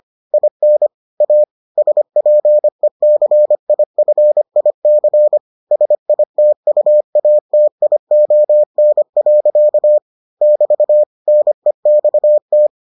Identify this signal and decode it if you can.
Morse code, heard as a keyed tone, about 25 wpm, 600 Hz. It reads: IN A SPECIFIC SITUATION. <BT> NEXT